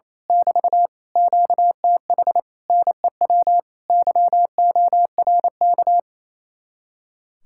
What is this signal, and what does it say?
Morse code, heard as a keyed tone, about 26 wpm, 700 Hz. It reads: <BT> QTH NEW YORK
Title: Speed-Racing Common QSO Phrases Encore - 28wpm